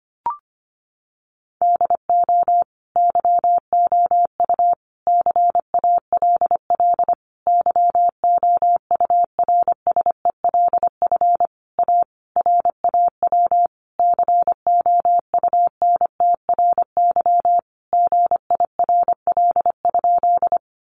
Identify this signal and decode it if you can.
Morse code, heard as a keyed tone, about 25 wpm, 700 Hz. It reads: DO YOU CALL YOURSELF A RAW COUNTRY GIRL?
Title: Sentences from Top 1500 Words 25wpm